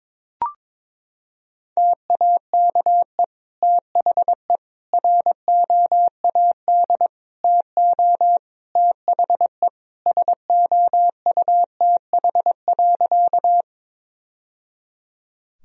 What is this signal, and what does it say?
Morse code, heard as a keyed tone, about 22 wpm, 700 Hz. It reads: TAKE THE ROAD TO THE SOUTH.
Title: Sentences from Top 1000 Words - Review Part 2 - 22wpm